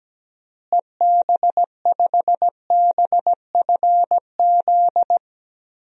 Morse code, heard as a keyed tone, about 17 wpm, 700 Hz. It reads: EB5BFZ
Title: DXCC-100 Call Signs - 6 Characters - Repeated 2x - 17wpm